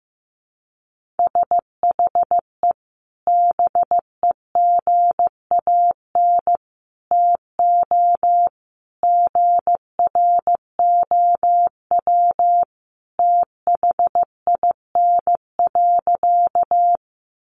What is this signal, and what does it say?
Morse code, heard as a keyed tone, about 15 wpm, 700 Hz. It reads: SHE BEGAN TO GROW THIN.
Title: Sentences from Top 1200 Words 15wpm